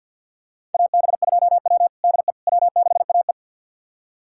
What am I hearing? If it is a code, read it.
Morse code, heard as a keyed tone, about 51 wpm, 690 Hz. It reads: AB2W DE W6AE
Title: USA Callsign DE Callsign - 4 Characters - 50wpm